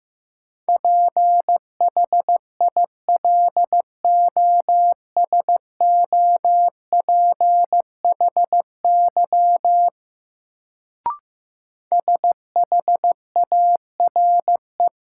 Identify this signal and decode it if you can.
Morse code, heard as a keyed tone, about 15 wpm, 700 Hz. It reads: PHILOSOPHY SHARE
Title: Sentences from Top 900 Words - New Word Drill - 15wpm